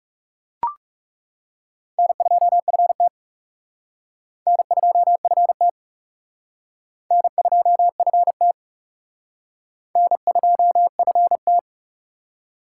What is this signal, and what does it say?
Morse code, heard as a keyed tone, about 36 wpm, 700 Hz. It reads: N2FT N2FT N2FT N2FT
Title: Speed Racing - USA 1x2 Call Signs 30wpm